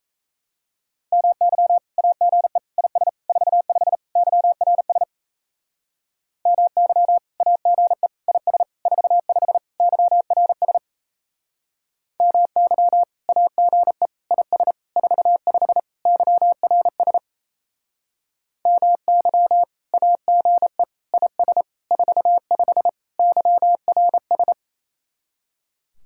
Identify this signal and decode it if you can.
Morse code, heard as a keyed tone, about 34 wpm, 700 Hz. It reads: MY AGE IS 45 YRS MY AGE IS 45 YRS MY AGE IS 45 YRS MY AGE IS 45 YRS
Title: Speed-Racing Common QSO Phrases - 28wpm